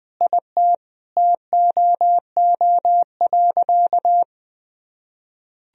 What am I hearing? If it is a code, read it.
Morse code, heard as a keyed tone, about 20 wpm, 700 Hz. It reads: IT TOO.
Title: Sentences from Top 1000 Words - Review Part 1 - 20wpm